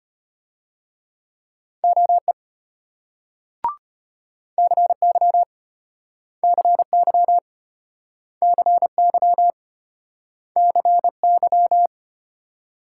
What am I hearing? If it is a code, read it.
Morse code, heard as a keyed tone, about 34 wpm, 700 Hz. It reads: OE CY CY CY CY